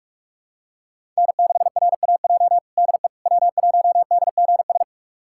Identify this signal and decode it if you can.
Morse code, heard as a keyed tone, about 46 wpm, 700 Hz. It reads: N6RAJ DE W1DGS
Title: USA Callsign DE Callsign - 5 Characters - 45wpm